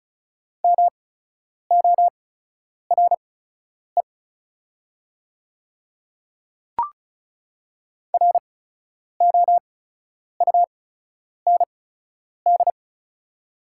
Morse code, heard as a keyed tone, about 36 wpm, spaced as for 10 wpm, 690 Hz. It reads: MORE ROUND
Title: 8x Character Spacing - Top 200 Words - 35wpm